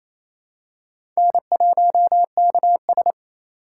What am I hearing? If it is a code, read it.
Morse code, heard as a keyed tone, about 28 wpm, 700 Hz. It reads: N1KS
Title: USA Call Signs - 4 Characters - 3x Character Spacing - 28wpm